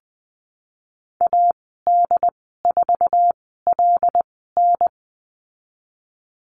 Morse code, heard as a keyed tone, about 19 wpm, spaced as for 14 wpm, 700 Hz. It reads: AD4LN
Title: 076 Lesson 13g - Letter L - Call Signs 20wpm